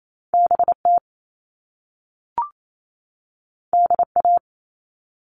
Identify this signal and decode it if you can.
Morse code, heard as a keyed tone, about 28 wpm, 700 Hz. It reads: BT DA
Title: All Cut Numbers - 2-Digit 28wpm